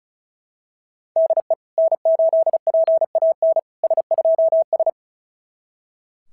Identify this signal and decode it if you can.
Morse code, heard as a keyed tone, about 36 wpm, 690 Hz. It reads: DE N8PAN S2S
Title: SOTA QSO Practice 35wpm